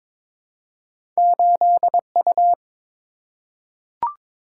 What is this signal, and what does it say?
Morse code, heard as a keyed tone, about 22 wpm, 700 Hz. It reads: 8U
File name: practice / 22wpm / Two Letter-Number Combinations - 22wpm